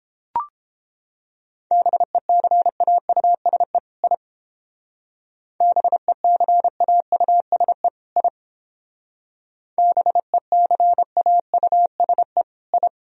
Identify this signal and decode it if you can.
Morse code, heard as a keyed tone, about 30 wpm, 700 Hz. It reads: BECAUSE I BECAUSE I BECAUSE I